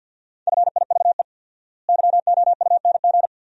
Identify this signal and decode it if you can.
Morse code, heard as a keyed tone, about 50 wpm, 700 Hz. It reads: FIVE YOUNG
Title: 4x Word-Spacing - Sentences from Top 400 Words - 50wpm